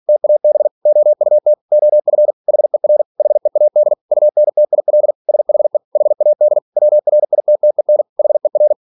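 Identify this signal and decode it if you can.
Morse code, heard as a keyed tone, about 46 wpm, 600 Hz. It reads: TAB OUT OF HER HEAD UNTIL SHE HAD WRITTEN HER
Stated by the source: Practice text from an English novel sent at 47 WPM.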